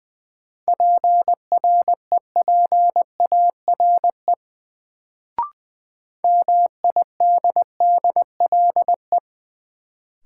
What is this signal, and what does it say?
Morse code, heard as a keyed tone, about 20 wpm, 700 Hz. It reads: PREPARE MIDDLE
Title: Sentences from Top 900 Words - New Word Drill - 20wpm